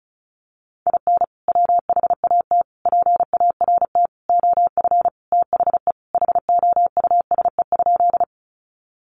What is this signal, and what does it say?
Morse code, heard as a keyed tone, about 34 wpm, 700 Hz. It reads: IN WHAT PART OF THE HOUSE?
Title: Sentences from Top 200 Words 35wpm